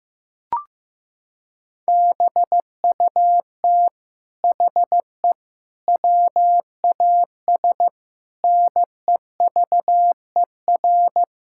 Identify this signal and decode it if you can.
Morse code, heard as a keyed tone, about 15 wpm, 700 Hz. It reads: BUT HE WAS NEVER
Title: Sentences from Top 1000 Words - Review Part 5 - 15wpm